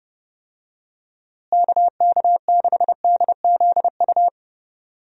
Morse code, heard as a keyed tone, about 32 wpm, 700 Hz. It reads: KK6DZU
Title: USA Call Signs - 6 Characters - Repeated 3x - 30wpm